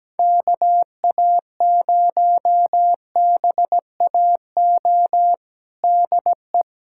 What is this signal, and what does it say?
Morse code, heard as a keyed tone, about 17 wpm, 700 Hz. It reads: KA0BAO DE
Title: USA Callsign DE Callsign 17wpm